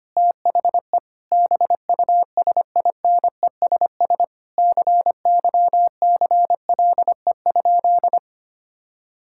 Morse code, heard as a keyed tone, about 25 wpm, 700 Hz. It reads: THE BUSINESS CYCLE?
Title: Sentences from Top 1000 Words - Review Part 6 - 25wpm